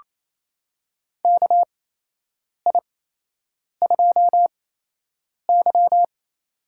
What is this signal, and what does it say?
Morse code, heard as a keyed tone, about 28 wpm, spaced as for 8 wpm, 700 Hz. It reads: KI2Y
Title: USA Call Signs - 4 Characters - 8x Character Spacing - 28wpm